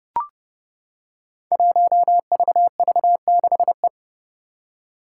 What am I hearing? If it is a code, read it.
Morse code, heard as a keyed tone, about 30 wpm, 700 Hz. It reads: 1VV6E
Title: Five Challenging Letter-Number Combinations 30wpm